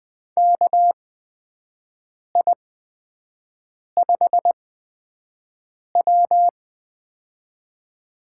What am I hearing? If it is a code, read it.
Morse code, heard as a keyed tone, about 20 wpm, spaced as for 5 wpm, 700 Hz. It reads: KI5W
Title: USA Call Signs - 4 Characters - 8x Character Spacing - 20wpm